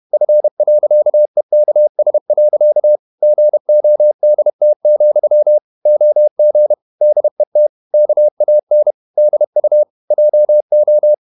Morse code, heard as a keyed tone, about 31 wpm, 600 Hz. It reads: F.EKS. GODT, OG DET KAN DU JO
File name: Ferie QRQ 31wpm